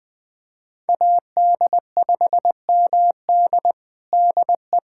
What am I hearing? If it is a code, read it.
Morse code, heard as a keyed tone, about 20 wpm, 700 Hz. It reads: AD5MD DE
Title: USA Callsign DE Callsign - 5 Characters - 20wpm